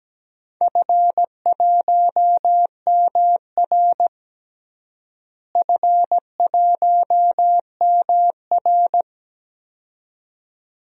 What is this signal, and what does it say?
Morse code, heard as a keyed tone, about 17 wpm, 700 Hz. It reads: F1MR F1MR
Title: DXCC-100 Call Signs - 4 Characters - Repeated 2x - 17wpm